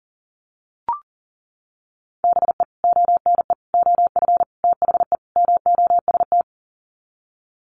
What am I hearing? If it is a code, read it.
Morse code, heard as a keyed tone, about 40 wpm, 700 Hz. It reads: BE ONE OF THE MOST
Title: Sets of 5 Words Encore 40wpm